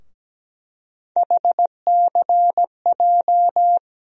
Morse code, heard as a keyed tone, about 17 wpm, 700 Hz. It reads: HCJ
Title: USA Call Sign Postfixes - 3 Letters 17wpm